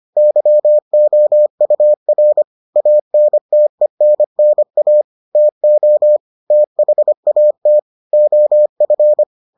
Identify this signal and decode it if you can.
Morse code, heard as a keyed tone, about 25 wpm, 600 Hz. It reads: YOUR ANTENNA TO THAT OF